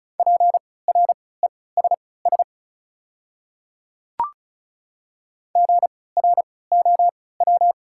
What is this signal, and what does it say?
Morse code, heard as a keyed tone, about 36 wpm, spaced as for 20 wpm, 700 Hz. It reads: PRESS GROW
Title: 3x Character Spacing - Top 300 Words - 35wpm